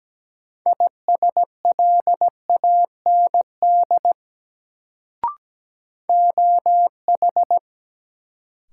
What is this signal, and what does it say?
Morse code, heard as a keyed tone, about 17 wpm, 700 Hz. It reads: ISLAND OH
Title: Sentences from Top 500 Words - New Word Drill - 17wpm